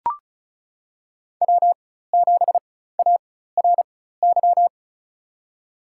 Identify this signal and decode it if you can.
Morse code, heard as a keyed tone, about 36 wpm, spaced as for 16 wpm, 700 Hz. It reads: W7ARY
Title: USA Call Signs - 5 Characters - 4x Character Spacing - 35wpm